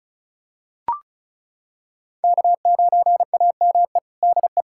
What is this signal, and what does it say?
Morse code, heard as a keyed tone, about 34 wpm, 700 Hz. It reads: K9AME DE
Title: USA Callsign DE Callsign - 5 Characters - 35wpm